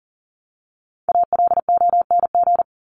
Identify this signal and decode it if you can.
Morse code, heard as a keyed tone, about 40 wpm, 700 Hz. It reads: ALONG
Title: Top 501-750 Words 40wpm